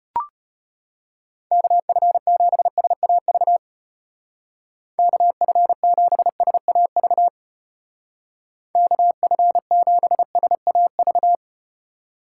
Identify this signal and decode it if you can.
Morse code, heard as a keyed tone, about 34 wpm, 700 Hz. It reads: KF7SAV KF7SAV KF7SAV